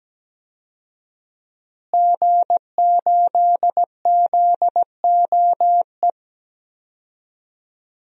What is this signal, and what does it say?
Morse code, heard as a keyed tone, about 17 wpm, 700 Hz. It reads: G8ZOE
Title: DXCC-100 Call Signs - 5 Characters - Repeated 2x - 17wpm